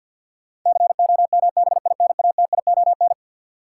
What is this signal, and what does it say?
Morse code, heard as a keyed tone, about 49 wpm, 700 Hz. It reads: COMBINATION
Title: Sentences from Top 1000 Words - Review Part 6 - New Word Drill - 50wpm